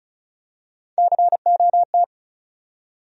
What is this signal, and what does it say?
Morse code, heard as a keyed tone, about 36 wpm, 700 Hz. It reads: COT
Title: USA Call Sign Postfixes - 3 Letters 35wpm